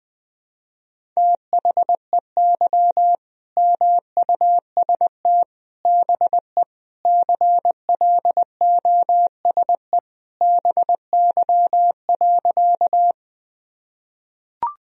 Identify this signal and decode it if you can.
Morse code, heard as a keyed tone, about 21 wpm, 700 Hz. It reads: THEY MUST BE CLOSE BY.
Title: Sentences from Top 1000 Words - Review Part 1 - 20wpm